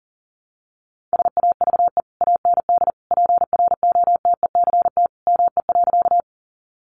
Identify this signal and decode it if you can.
Morse code, heard as a keyed tone, about 40 wpm, 700 Hz. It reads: SAVE AND PROTECT ME.
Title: Sentences from Top 1100 Words 40wpm